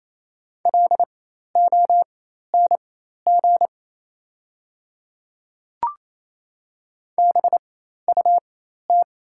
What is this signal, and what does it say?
Morse code, heard as a keyed tone, about 27 wpm, spaced as for 13 wpm, 700 Hz. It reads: LONG BUT
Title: 4x Character Spacing - Top 500 Words - 28wpm